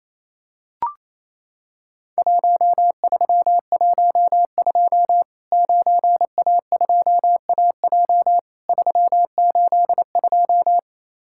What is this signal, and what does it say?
Morse code, heard as a keyed tone, about 27 wpm, 700 Hz. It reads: 1312 9A2AJ 382
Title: Worked All Europe DX Contest - QTC Exchange - 28wpm